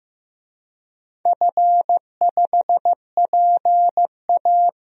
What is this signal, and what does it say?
Morse code, heard as a keyed tone, about 15 wpm, 700 Hz. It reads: F5PA
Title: DXCC-100 Call Signs - 4 Characters - Repeated 2x - 15wpm